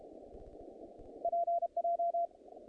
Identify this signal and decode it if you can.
Morse code, heard as a keyed tone, about 32 wpm, 690 Hz. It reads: PJ
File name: contests / naqp-cw-k5tr-op-k5pi- -2300